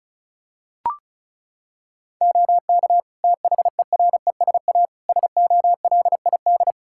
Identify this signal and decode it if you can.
Morse code, heard as a keyed tone, about 36 wpm, 700 Hz. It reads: OK THERESA SOLID
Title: Practice QSO - Exchange 3 - More Info - 35wpm